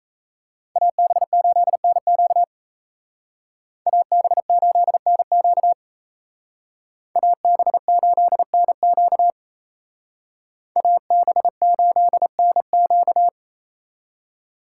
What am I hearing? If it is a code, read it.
Morse code, heard as a keyed tone, about 32 wpm, 700 Hz. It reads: AB8NQ AB8NQ AB8NQ AB8NQ
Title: Speed Racing - USA 2x2 Call Signs 28wpm